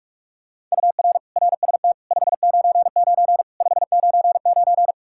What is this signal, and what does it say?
Morse code, heard as a keyed tone, about 41 wpm, 700 Hz. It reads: UR RST 599 599
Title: Practice QSO - Exchange 1 - Beginning - 45wpm